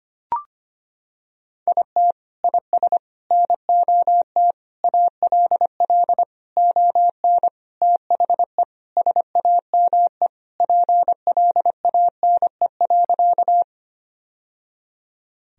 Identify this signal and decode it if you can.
Morse code, heard as a keyed tone, about 25 wpm, 700 Hz. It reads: IT IS NOT ALL ON THE SAME PLANE.
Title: Sentences from Top 1000 Words - Review Part 3 - 25wpm